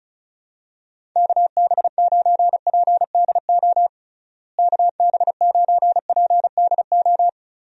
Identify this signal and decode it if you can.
Morse code, heard as a keyed tone, about 34 wpm, 700 Hz. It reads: KB9PDO KB9PDO
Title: USA Call Signs - 6 Characters - Repeated 3x - 35wpm